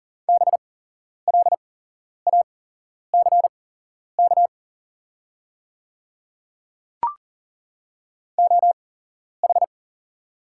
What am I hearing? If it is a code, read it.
Morse code, heard as a keyed tone, about 40 wpm, spaced as for 11 wpm, 700 Hz. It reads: BLACK OH
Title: 8x Character Spacing - Top 500 Words - 40wpm